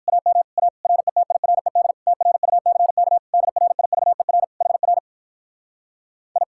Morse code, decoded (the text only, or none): AM A RETIRED TRUCK DRIVER HR I